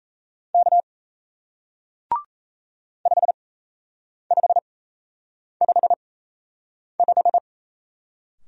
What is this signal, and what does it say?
Morse code, heard as a keyed tone, about 38 wpm, spaced as for 8 wpm, 700 Hz. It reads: K 5555